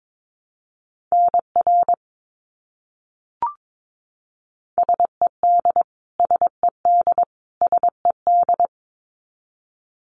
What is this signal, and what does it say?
Morse code, heard as a keyed tone, about 22 wpm, 700 Hz. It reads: NR SED SED SED
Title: Abbreviations Repeated 3x - 22wpm